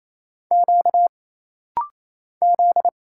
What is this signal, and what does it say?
Morse code, heard as a keyed tone, about 28 wpm, 700 Hz. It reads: Q Z
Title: Single Letter-Number - Rapid-Fire 28wpm